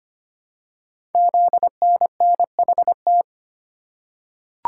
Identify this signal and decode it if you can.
Morse code, heard as a keyed tone, about 25 wpm, 700 Hz. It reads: ZNNHT
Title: Code Groups - 5 Letters - 25wpm